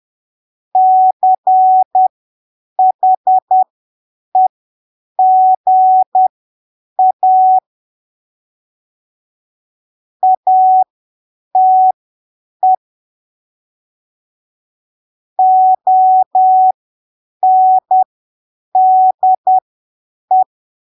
Morse code, heard as a keyed tone, about 10 wpm, spaced as for 7 wpm, 750 Hz. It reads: CHEGA ATE ONDE
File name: arquivos / texto / texto 13 10ppm